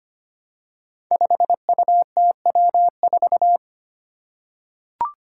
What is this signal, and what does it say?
Morse code, heard as a keyed tone, about 25 wpm, 700 Hz. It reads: 5UTW4
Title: Five Challenging Letter-Number Combinations 25wpm